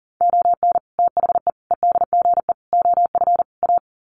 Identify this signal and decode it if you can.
Morse code, heard as a keyed tone, about 40 wpm, 700 Hz. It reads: ON THE EDGE OF A
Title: Sets of 5 Words Encore2 40wpm